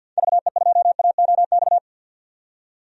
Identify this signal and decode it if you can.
Morse code, heard as a keyed tone, about 50 wpm, 700 Hz. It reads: VE2AOX
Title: DXCC-100 Call Signs - 6 Characters - 50wpm